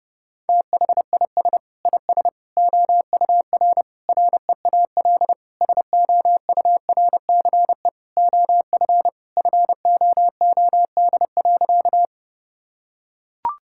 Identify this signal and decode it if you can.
Morse code, heard as a keyed tone, about 32 wpm, 700 Hz. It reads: THIS IS OUR REAL SOURCE OF FOOD.
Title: Sentences from Top 1000 Words - Review Part 5 - 30wpm